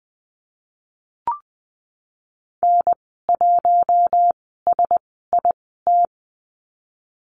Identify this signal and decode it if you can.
Morse code, heard as a keyed tone, about 20 wpm, spaced as for 15 wpm, 700 Hz. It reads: N1SIT
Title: 042 Lesson 8g - Number 1 - Call Signs 20wpm